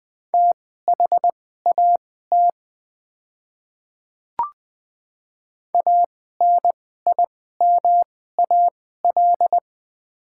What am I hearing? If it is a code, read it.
Morse code, heard as a keyed tone, about 20 wpm, spaced as for 14 wpm, 700 Hz. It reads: THAT ANIMAL
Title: 2x Character Spacing - Top 500 Words - 20wpm